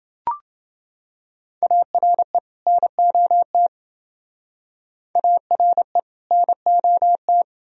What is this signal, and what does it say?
Morse code, heard as a keyed tone, about 29 wpm, 700 Hz. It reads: ARE NOT ARE NOT